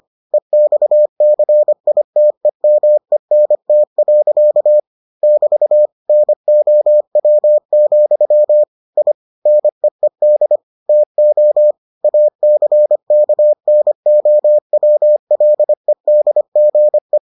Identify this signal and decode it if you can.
Morse code, heard as a keyed tone, about 25 wpm, 600 Hz. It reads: EXCITEMENT. <BT> NOW, I NEED TO ACKNOWLEDGE